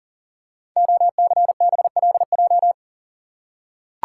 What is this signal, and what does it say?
Morse code, heard as a keyed tone, about 39 wpm, 690 Hz. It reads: OCBLJ
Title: Code Groups - 5 Letters - 40wpm